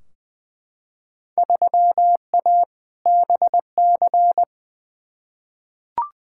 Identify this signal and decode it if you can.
Morse code, heard as a keyed tone, about 21 wpm, 700 Hz. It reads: 3A BC
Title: Field Day Exchanges - 20wpm